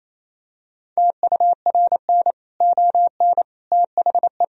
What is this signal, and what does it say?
Morse code, heard as a keyed tone, about 28 wpm, 700 Hz. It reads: TURN ON THE
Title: Sentences from Top 1000 Words - Review Part 4 - 28wpm